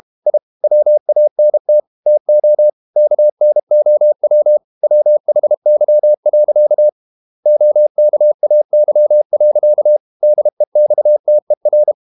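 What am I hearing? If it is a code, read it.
Morse code, heard as a keyed tone, about 32 wpm, 600 Hz. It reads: I WANT TO KNOW WHY. OKAY. DEXTER
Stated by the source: Morse code practice at 32 wpm from CW med Gnister, reading Ken Follett's novel Never in English.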